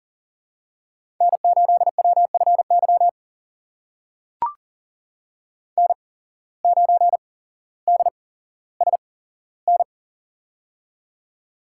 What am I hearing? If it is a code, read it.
Morse code, heard as a keyed tone, about 40 wpm, spaced as for 17 wpm, 700 Hz. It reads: N8WFY N9DSN
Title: USA Call Signs - 5 Characters - 8x Character Spacing - 40wpm